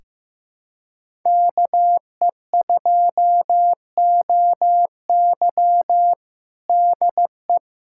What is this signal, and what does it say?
Morse code, heard as a keyed tone, about 16 wpm, 700 Hz. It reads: KE2OY DE
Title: USA Callsign DE Callsign 15wpm